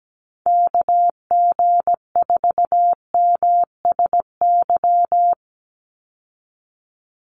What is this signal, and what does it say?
Morse code, heard as a keyed tone, about 17 wpm, 700 Hz. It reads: KG4MSY
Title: USA Call Signs 17wpm